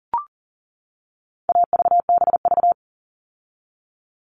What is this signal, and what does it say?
Morse code, heard as a keyed tone, about 40 wpm, 700 Hz. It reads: AVBV
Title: All Cut Numbers - 4-Digit 40wpm